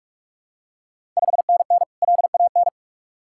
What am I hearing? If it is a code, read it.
Morse code, heard as a keyed tone, about 41 wpm, 700 Hz. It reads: 5NN LAN
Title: Ontario QSO Party Exchange 45wpm